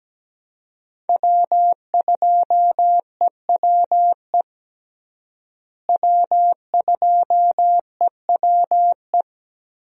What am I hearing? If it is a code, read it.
Morse code, heard as a keyed tone, about 17 wpm, 700 Hz. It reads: W2EWE W2EWE
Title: USA Call Signs - 5 Characters - Repeated 3x - 17wpm